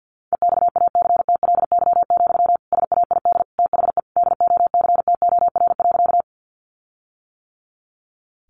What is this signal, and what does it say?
Morse code, heard as a keyed tone, about 50 wpm, 700 Hz. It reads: EXACTLY, SAID THE DOCTOR.
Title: Sentences from Top 1500 Words 50wpm